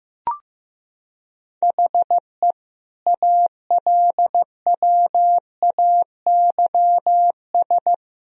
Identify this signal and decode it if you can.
Morse code, heard as a keyed tone, about 15 wpm, 700 Hz. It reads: HE ALWAYS
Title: Sentences from Top 1000 Words - Review Part 3 - 15wpm